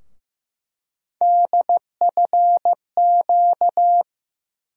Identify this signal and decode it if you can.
Morse code, heard as a keyed tone, about 15 wpm, 700 Hz. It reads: DFQ
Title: USA Call Sign Postfixes - 3 Letters 15wpm